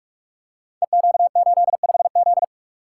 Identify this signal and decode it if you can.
Morse code, heard as a keyed tone, about 44 wpm, 700 Hz. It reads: EQ85Z
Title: Code Groups - 5 Letter:Numbers - 45wpm